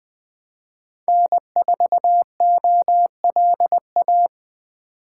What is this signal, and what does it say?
Morse code, heard as a keyed tone, about 20 wpm, 700 Hz. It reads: N4OLA
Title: Speed Racing - USA Call Signs 17wpm